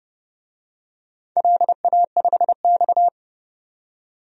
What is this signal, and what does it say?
Morse code, heard as a keyed tone, about 30 wpm, 700 Hz. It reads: LA5X
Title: DXCC-100 Call Signs - 4 Characters - 30wpm